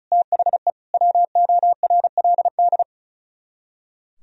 Morse code, heard as a keyed tone, about 43 wpm, 700 Hz. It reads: THE WORLD
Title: Speed-Racing - Sets of 5 Words 35wpm